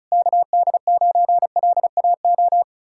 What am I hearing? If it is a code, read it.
Morse code, heard as a keyed tone, about 40 wpm, 700 Hz. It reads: KD9LAO